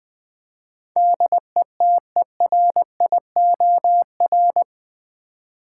Morse code, heard as a keyed tone, about 20 wpm, 700 Hz. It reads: DETERIOR
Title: Build A Word - 4 Letters At A Time - 11-Letter Words - 20wpm